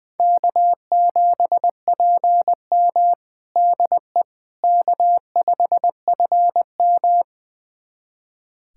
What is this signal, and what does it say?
Morse code, heard as a keyed tone, about 20 wpm, 700 Hz. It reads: K7PM DE K5FM
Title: USA Callsign DE Callsign - 4 Characters - 20wpm